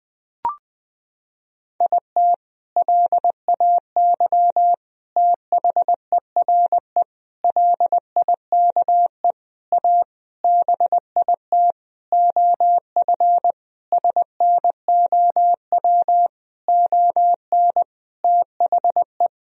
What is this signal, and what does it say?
Morse code, heard as a keyed tone, about 20 wpm, 700 Hz. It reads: IT LAY THERE LIKE A BIT OF SNOW ON THE
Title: Sentences from Top 1000 Words - Review Part 3 - 20wpm